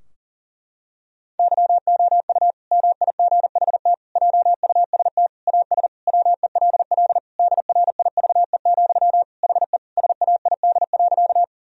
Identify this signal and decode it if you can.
Morse code, heard as a keyed tone, about 40 wpm, 700 Hz. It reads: YOU MIGHT JUST AS WELL DRIVE, HE SAID.
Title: Sentences from Top 1000 Words - Review Part 2 - 40wpm